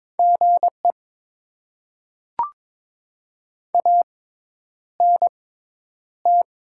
Morse code, heard as a keyed tone, about 22 wpm, spaced as for 10 wpm, 700 Hz. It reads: GE ANT
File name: Top 20 QSO Elements - 6x Character Spacing - 22wpm